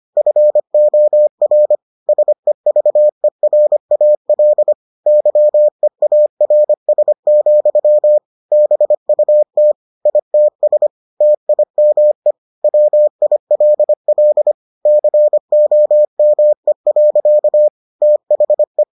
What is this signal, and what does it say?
Morse code, heard as a keyed tone, about 25 wpm, 600 Hz. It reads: FOR SEVERAL YEARS, BUT ITS TIME WILL COME. THE